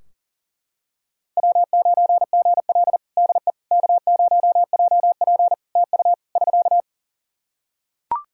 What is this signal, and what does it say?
Morse code, heard as a keyed tone, about 38 wpm, 700 Hz. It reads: W9GL DE K0JP TU <SK>
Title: Practice QSO - Exchange 4 - Ending - 40wpm